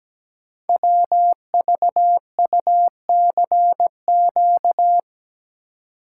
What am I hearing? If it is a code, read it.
Morse code, heard as a keyed tone, about 17 wpm, 700 Hz. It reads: WVUCQ
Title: Code Groups - 5 Letters - 17wpm